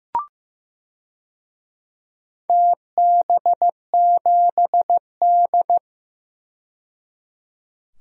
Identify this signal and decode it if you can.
Morse code, heard as a keyed tone, about 15 wpm, 700 Hz. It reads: TB7D
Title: DXCC-100 Call Signs - 4 Characters - 15wpm